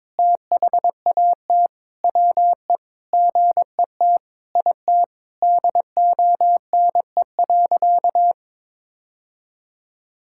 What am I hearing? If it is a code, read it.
Morse code, heard as a keyed tone, about 22 wpm, 700 Hz. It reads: THAT WE GET IT DONE.
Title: Sentences from Top 1600 Words 22wpm